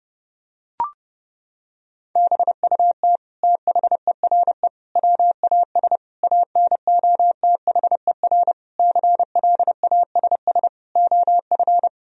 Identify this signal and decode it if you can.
Morse code, heard as a keyed tone, about 31 wpm, 700 Hz. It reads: BUT THERE WAS ANOTHER CLASS OF
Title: Sentences from Top 1000 Words - Review Part 2 - 30wpm